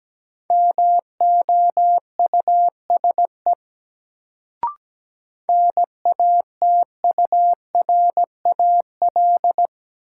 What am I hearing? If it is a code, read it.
Morse code, heard as a keyed tone, about 17 wpm, 700 Hz. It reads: MOUSE NATURAL
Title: Sentences from Top 700 Words - New Word Drill - 17wpm